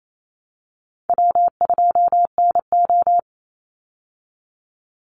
Morse code, heard as a keyed tone, about 28 wpm, 700 Hz. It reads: W2NO
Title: Easy Call Signs Repeated 2x - 28wpm